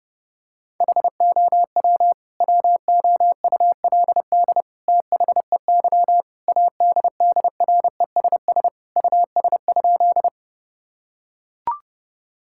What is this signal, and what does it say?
Morse code, heard as a keyed tone, about 30 wpm, 700 Hz. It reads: HOW WOULD THEY ADDRESS US?
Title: Sentences from Top 1000 Words - Review Part 6 - 30wpm